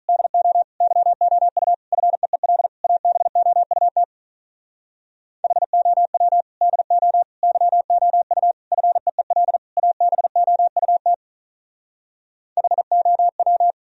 Morse code, heard as a keyed tone, about 46 wpm, 700 Hz. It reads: DO YOU FEEL ABOUT HOW DO YOU FEEL ABOUT HOW